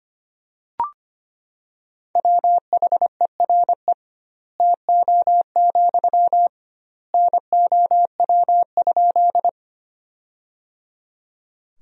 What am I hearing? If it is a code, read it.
Morse code, heard as a keyed tone, about 25 wpm, 700 Hz. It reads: WHERE TO, NOW?
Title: 2x Word-Spacing - Sentences from Top 200 Words - 25wpm